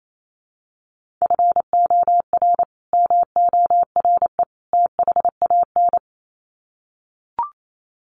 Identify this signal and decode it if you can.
Morse code, heard as a keyed tone, about 28 wpm, 700 Hz. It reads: FOR MORE THAN
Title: Sets of 3 Words 28wpm